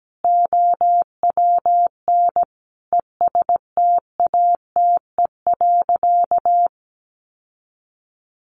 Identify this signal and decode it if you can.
Morse code, heard as a keyed tone, about 17 wpm, 700 Hz. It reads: OWN ESTATE.
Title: Sentences from Top 1200 Words 17wpm